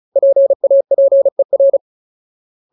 Morse code, heard as a keyed tone, about 34 wpm, 550 Hz. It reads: PAPER
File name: Ultimate Word List / 35WPM Ultimate